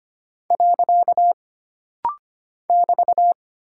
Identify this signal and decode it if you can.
Morse code, heard as a keyed tone, about 25 wpm, 700 Hz. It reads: . <BT>
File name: Single Punctuation-Prosign - Rapid-Fire - 25wpm